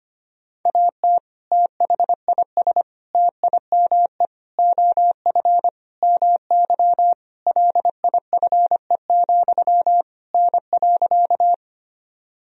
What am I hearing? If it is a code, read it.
Morse code, heard as a keyed tone, about 25 wpm, 700 Hz. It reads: AT THIS TIME OF MY LIFE, N.
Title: Sentences from Top 300 Words and Abbreviations 25wpm